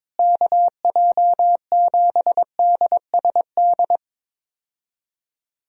Morse code, heard as a keyed tone, about 22 wpm, 700 Hz. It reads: KJ7DSD
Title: USA Call Signs - 6 Characters - Repeated 3x - 22wpm